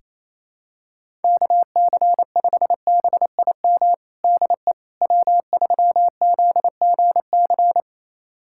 Morse code, heard as a keyed tone, about 27 wpm, 700 Hz. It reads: KC5BIM DE W3ZGC
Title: USA Callsign DE Callsign 28wpm